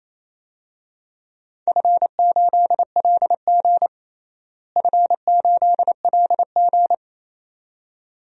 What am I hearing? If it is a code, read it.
Morse code, heard as a keyed tone, about 27 wpm, 700 Hz. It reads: F8LG F8LG
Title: DXCC-100 Call Signs - 4 Characters - Repeated 2x - 28wpm